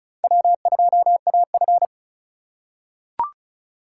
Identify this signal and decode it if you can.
Morse code, heard as a keyed tone, about 34 wpm, 700 Hz. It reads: W2AF
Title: USA Call Signs - 3x Character Spacing - 35wpm